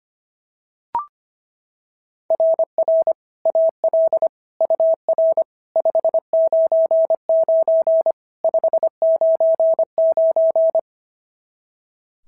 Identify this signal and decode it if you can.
Morse code, heard as a keyed tone, about 25 wpm, 690 Hz. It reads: RR AL UR 599 599
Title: SOTA QSO Practice 25wpm